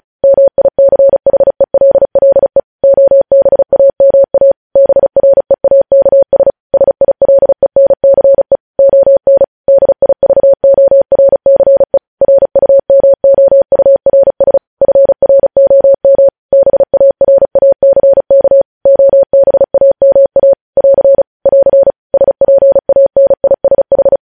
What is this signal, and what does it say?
Morse code, heard as a keyed tone, about 36 wpm, 550 Hz. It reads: MICHELLE OBAMA BREAKS SILENCE ON DIVORCE RUMOURS FROM BARACK OBAMA <AR> <AR> SPANISH
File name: podcast / News headlines-1x-35x35